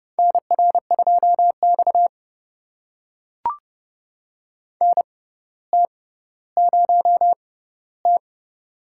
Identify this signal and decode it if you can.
Morse code, heard as a keyed tone, about 32 wpm, spaced as for 16 wpm, 700 Hz. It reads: NR2X NT0T
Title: USA Call Signs - 4 Characters - 6x Character Spacing - 30wpm